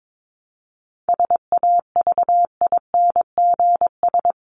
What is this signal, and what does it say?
Morse code, heard as a keyed tone, about 22 wpm, 700 Hz. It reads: SAVINGS
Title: Top 1001-1250 Words 22wpm